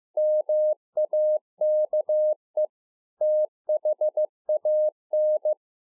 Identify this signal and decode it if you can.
Morse code, heard as a keyed tone, about 15 wpm, 610 Hz. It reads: MAKE THAN
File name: 2011-04-mp3 / A SMALL TRANCEIVER 2 15WPM